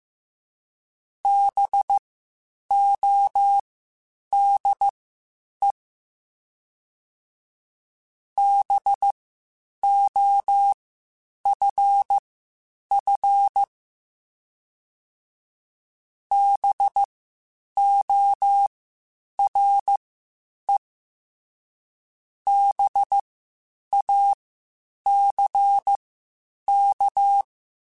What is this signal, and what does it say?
Morse code, heard as a keyed tone, about 15 wpm, spaced as for 8 wpm, 800 Hz. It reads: BODE BOFF BORE BACK